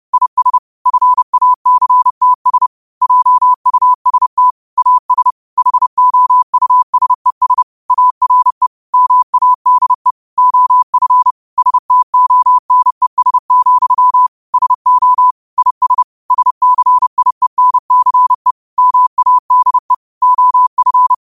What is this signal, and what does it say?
Morse code, heard as a keyed tone, about 30 wpm, 1000 Hz. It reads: IS FACTS JUST AS HOUSES ARE MADE OF STONES, SO IS SCIENCE MADE OF
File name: qotdfiles / 30wpm